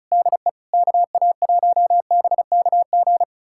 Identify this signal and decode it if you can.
Morse code, heard as a keyed tone, about 34 wpm, 700 Hz. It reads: DE KA1BKG
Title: USA Callsign DE Callsign - 6 Characters - 35wpm